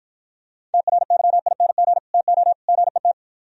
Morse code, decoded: TRYING TO GET

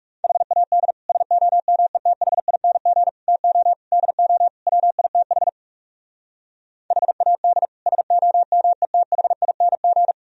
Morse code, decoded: HAD SOMETHING TO DO WITH HAD SOMETHING

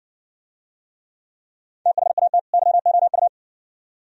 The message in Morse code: THAT YOU

